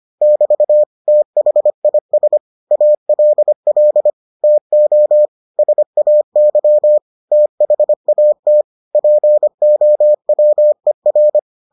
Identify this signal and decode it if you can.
Morse code, heard as a keyed tone, about 25 wpm, 600 Hz. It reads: <BT> THIS ALL TO SAY THAT POWER